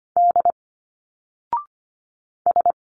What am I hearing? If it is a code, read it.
Morse code, heard as a keyed tone, about 25 wpm, 700 Hz. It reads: D S
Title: Single Letters 25wpm